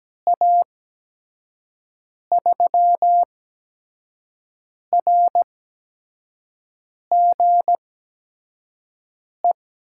Morse code, heard as a keyed tone, about 17 wpm, spaced as for 5 wpm, 700 Hz. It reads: A3RGE
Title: USA Call Signs - 6 Characters - 8x Character Spacing - 17wpm